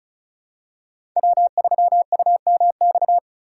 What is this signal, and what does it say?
Morse code, heard as a keyed tone, about 34 wpm, 700 Hz. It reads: W3UMX
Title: USA Call Signs - 5 Characters - 4x Character Spacing - 35wpm